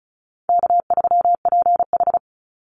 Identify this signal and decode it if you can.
Morse code, heard as a keyed tone, about 34 wpm, 700 Hz. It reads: K3PH
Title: Call Signs - 35wpm